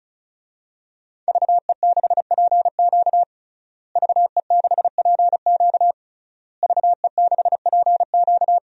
Morse code, heard as a keyed tone, about 36 wpm, 700 Hz. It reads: VE6PQ VE6PQ VE6PQ
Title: DXCC-100 Call Signs - 5 Characters - Repeated 3x - 35wpm